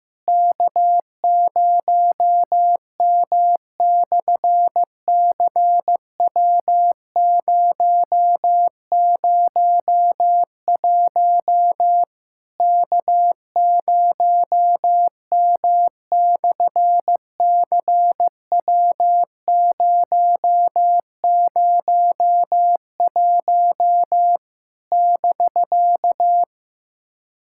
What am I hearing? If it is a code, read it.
Morse code, heard as a keyed tone, about 15 wpm, 700 Hz. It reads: K0M/CW001 K0M/CW001 <BK>
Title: SOTA QSO Practice 15wpm